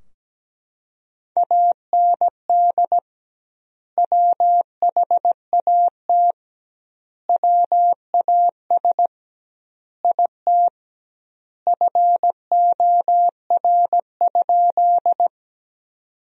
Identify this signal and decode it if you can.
Morse code, heard as a keyed tone, about 17 wpm, 700 Hz. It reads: AND WHAT WAS IT FOR?
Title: 2x Word-Spacing - Sentences from Top 100 Words - 17wpm